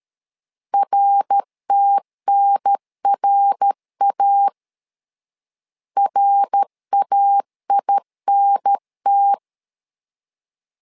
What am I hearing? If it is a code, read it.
Morse code, heard as a keyed tone, about 13 wpm, 800 Hz. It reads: RTNRA RAINT
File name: LEZIONI / 3 MWRO / 3 MWR 12wpm 1711